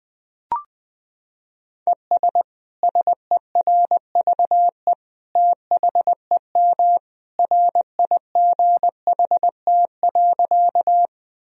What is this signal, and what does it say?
Morse code, heard as a keyed tone, about 20 wpm, 700 Hz. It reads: ES SERVE THEM RIGHT.
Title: Sentences from Top 400 Words and Abbreviations 20wpm